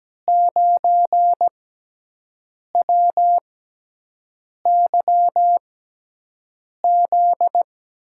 Morse code, heard as a keyed tone, about 17 wpm, spaced as for 6 wpm, 700 Hz. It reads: 9WYZ
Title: USA Call Signs - 6 Characters - 6x Character Spacing - 17wpm